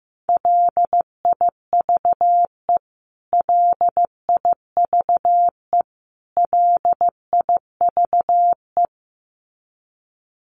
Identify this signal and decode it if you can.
Morse code, heard as a keyed tone, about 15 wpm, 700 Hz. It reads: LIVE LIVE LIVE
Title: Top 500 Words Repeated 15wpm